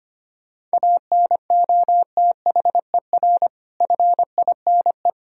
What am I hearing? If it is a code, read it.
Morse code, heard as a keyed tone, about 25 wpm, 700 Hz. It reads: ANOTHER FINE
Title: Sentences from Top 1000 Words - Review Part 1 - 25wpm